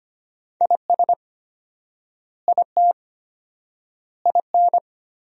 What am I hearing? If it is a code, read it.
Morse code, heard as a keyed tone, about 25 wpm, 700 Hz. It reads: IS IT IN
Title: 4x Word-Spacing - Sentences from Top 300 Words - 25wpm